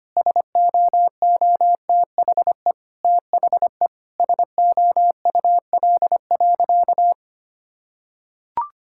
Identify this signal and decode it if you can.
Morse code, heard as a keyed tone, about 25 wpm, 700 Hz. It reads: SOOTHE THE SOUL.
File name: practice / 25wpm / Difficult Sentences with Hints - 25wpm